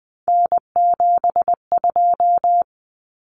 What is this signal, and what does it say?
Morse code, heard as a keyed tone, about 20 wpm, 700 Hz. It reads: N72
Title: Common Cut Numbers - 3-Digit 20wpm